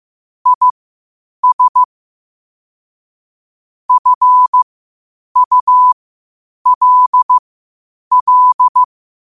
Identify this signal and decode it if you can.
Morse code, heard as a keyed tone, about 15 wpm, spaced as for 8 wpm, 1000 Hz. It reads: IS FULL
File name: qotdfiles / 8wpm